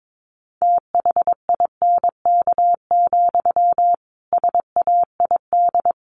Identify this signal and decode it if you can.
Morse code, heard as a keyed tone, about 22 wpm, 700 Hz. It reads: THINK, SAID
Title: A Sound Of Thunder 22wpm